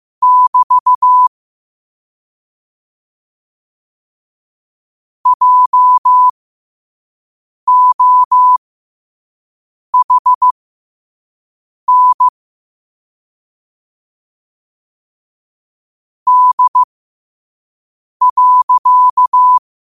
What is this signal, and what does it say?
Morse code, heard as a keyed tone, about 15 wpm, spaced as for 5 wpm, 1000 Hz. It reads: <BT> JOHN D.
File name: qotdfiles / 5wpm